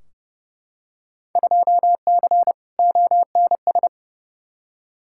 Morse code, heard as a keyed tone, about 30 wpm, 700 Hz. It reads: 2C ONS
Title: Field Day Exchanges Encore - 30wpm